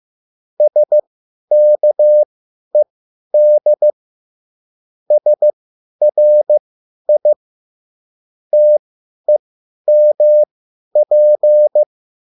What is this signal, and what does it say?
Morse code, heard as a keyed tone, about 15 wpm, spaced as for 10 wpm, 600 Hz. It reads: SKED SRI TEMP
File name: QSL / CW lesson